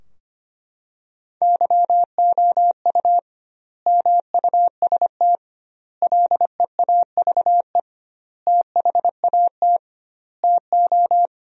Morse code, heard as a keyed tone, about 25 wpm, 700 Hz. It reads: YOU MUST LEAVE THAT TO
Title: 2x Word-Spacing - Sentences from Top 400 Words - 25wpm